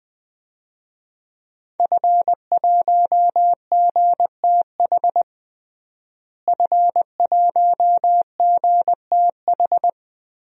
Morse code, heard as a keyed tone, about 20 wpm, 700 Hz. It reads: F1GTH F1GTH
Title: DXCC-100 Call Signs - 5 Characters - Repeated 3x - 20wpm